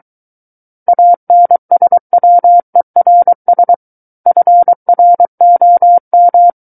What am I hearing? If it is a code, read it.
Morse code, heard as a keyed tone, about 23 wpm, 700 Hz. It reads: ANSWERS FROM
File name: Book One / 14-Chapter 14 - 23 WPM